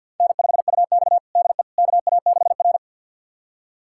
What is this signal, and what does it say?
Morse code, heard as a keyed tone, about 52 wpm, 700 Hz. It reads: N5UX DE KA6R
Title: USA Callsign DE Callsign - 4 Characters - 50wpm